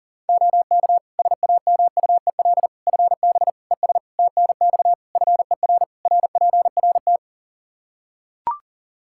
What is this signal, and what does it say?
Morse code, heard as a keyed tone, about 40 wpm, 700 Hz. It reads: OK SAMUEL FB ES TNX FER RPRT
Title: Practice QSO - Exchange 2 - Info - 40wpm